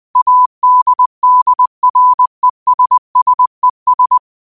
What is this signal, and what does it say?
Morse code, heard as a keyed tone, about 20 wpm, 1000 Hz. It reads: ADDRESSES
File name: qotdfiles / 20wpm